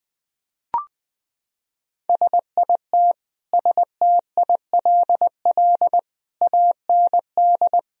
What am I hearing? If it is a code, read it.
Morse code, heard as a keyed tone, about 20 wpm, 700 Hz. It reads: SIT STILL AND
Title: Sentences from Top 1000 Words - Review Part 1 - 20wpm